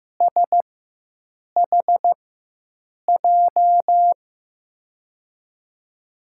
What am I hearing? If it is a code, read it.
Morse code, heard as a keyed tone, about 15 wpm, spaced as for 7 wpm, 700 Hz. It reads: SHJ
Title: USA Call Signs - 6 Characters - 4x Character Spacing - 15wpm